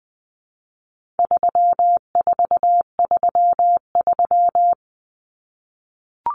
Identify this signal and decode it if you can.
Morse code, heard as a keyed tone, about 20 wpm, 700 Hz. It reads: 3433
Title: Numbers - 4-Digit 20wpm